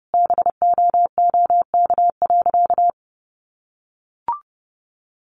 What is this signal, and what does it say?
Morse code, heard as a keyed tone, about 30 wpm, 700 Hz. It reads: BOOK.
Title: Sentences from Top 1200 Words 30wpm